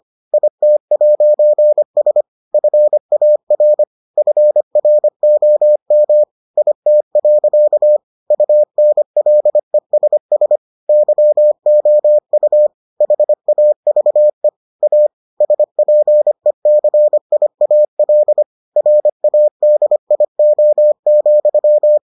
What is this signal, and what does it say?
Morse code, heard as a keyed tone, about 25 wpm, 600 Hz. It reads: IT'S FAR FROM IT. UNLESS YOU HAVE A SPECIAL RADIO,